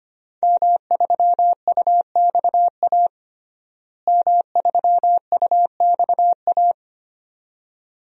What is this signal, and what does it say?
Morse code, heard as a keyed tone, about 25 wpm, 700 Hz. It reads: M3UXA M3UXA
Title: DXCC-100 Call Signs - 5 Characters - Repeated 2x - 25wpm